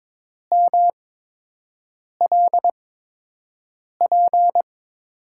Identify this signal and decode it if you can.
Morse code, heard as a keyed tone, about 22 wpm, spaced as for 6 wpm, 700 Hz. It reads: MLP
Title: USA Call Signs - 5 Characters - 8x Character Spacing - 22wpm